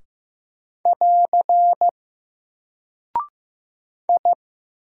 Morse code, heard as a keyed tone, about 15 wpm, 700 Hz. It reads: <AR> I
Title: Single Character (Letter:Number:Punctuation:Prosign) Alt 15wpm